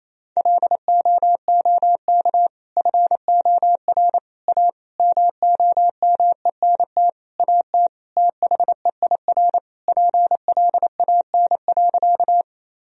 Morse code, heard as a keyed tone, about 28 wpm, 690 Hz. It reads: LOOK FOR A MOMENT AT THEIR PLAN.
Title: Sentences from Top 1000 Words - Review Part 2 - 28wpm